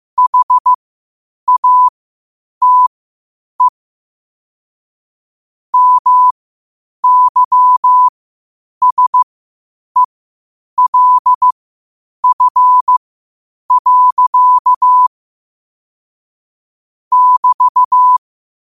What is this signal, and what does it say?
Morse code, heard as a keyed tone, about 15 wpm, spaced as for 8 wpm, 1000 Hz. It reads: HATE MYSELF. <BT>
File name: qotdfiles / 8wpm